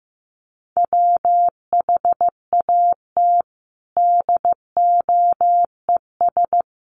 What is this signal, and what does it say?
Morse code, heard as a keyed tone, about 14 wpm, 700 Hz. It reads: WHAT DOES
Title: Sentences from Top 1100 Words 15wpm